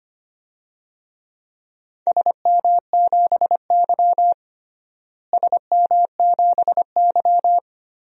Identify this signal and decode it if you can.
Morse code, heard as a keyed tone, about 25 wpm, 700 Hz. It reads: SM7Y SM7Y
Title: DXCC-100 Call Signs - 4 Characters - Repeated 2x - 25wpm